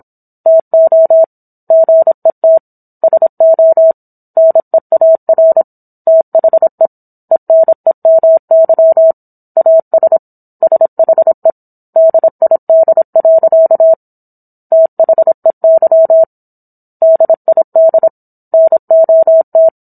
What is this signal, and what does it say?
Morse code, heard as a keyed tone, about 27 wpm, 650 Hz. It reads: TO GET SO NEAR THE ENEMY AS SHE DID. THEY DID NOT
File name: Book One / 17-Chapter 17 - 26 WPM